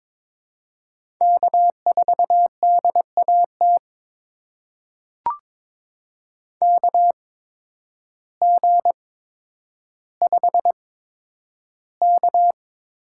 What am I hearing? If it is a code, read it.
Morse code, heard as a keyed tone, about 22 wpm, spaced as for 10 wpm, 700 Hz. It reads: K4DAT KG5K
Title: USA Call Signs - 8x Character Spacing - 22wpm